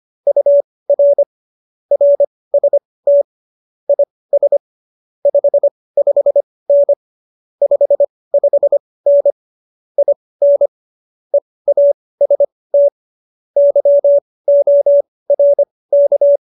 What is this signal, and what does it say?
Morse code, heard as a keyed tone, about 25 wpm, spaced as for 18 wpm, 570 Hz. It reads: UR RST IS 55N 55N IN EAST YORK